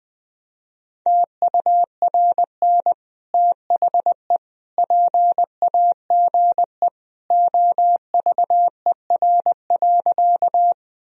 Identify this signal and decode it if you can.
Morse code, heard as a keyed tone, about 20 wpm, 700 Hz. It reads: TURN THE PAGE OVER.
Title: Sentences from Top 1000 Words - Review Part 1 - 20wpm